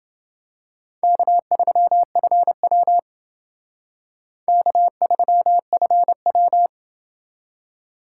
Morse code, heard as a keyed tone, about 28 wpm, 700 Hz. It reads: K3FW K3FW